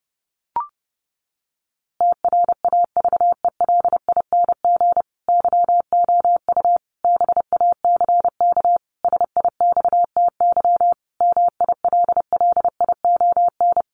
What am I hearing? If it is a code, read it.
Morse code, heard as a keyed tone, about 30 wpm, 700 Hz. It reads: TRAVELING YOU BACK SIXTY MILLION
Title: A Sound Of Thunder 30wpm